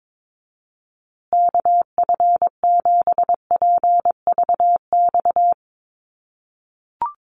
Morse code, heard as a keyed tone, about 22 wpm, 700 Hz. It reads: KF7PVX
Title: USA Call Signs Repeated 3x 22wpm